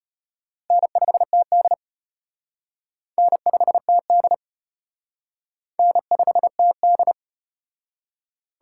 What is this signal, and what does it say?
Morse code, heard as a keyed tone, about 34 wpm, 700 Hz. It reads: N5TD N5TD N5TD